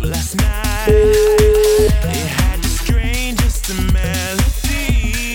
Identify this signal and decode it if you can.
Ringback tone (Europe, 425 Hz), with noise.